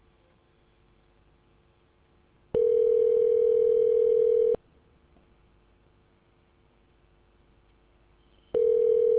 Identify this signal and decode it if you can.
Ringback tone (North America), 440 + 480 Hz.